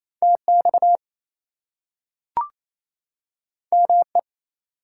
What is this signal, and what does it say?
Morse code, heard as a keyed tone, about 28 wpm, 700 Hz. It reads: TX ME
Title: US States and Canadian Provinces 28wpm